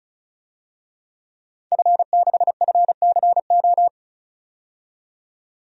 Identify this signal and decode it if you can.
Morse code, heard as a keyed tone, about 34 wpm, 700 Hz. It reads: F6FCO
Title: DXCC-100 Call Signs - 5 Characters - 35wpm